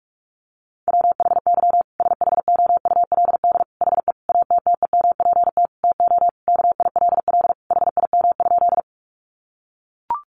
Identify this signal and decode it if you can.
Morse code, heard as a keyed tone, about 44 wpm, 700 Hz. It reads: WHY SHOULD HE ATTEMPT TO KILL HIM?
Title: Sentences from Top 1400 Words 45wpm